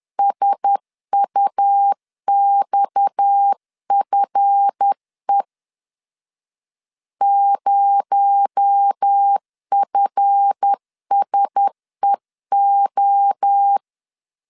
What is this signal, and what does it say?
Morse code, heard as a keyed tone, about 10 wpm, 800 Hz. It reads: SUXFE 0FSEO
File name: LEZIONI / 7 GF9Z / 7 GF 10wpm 2514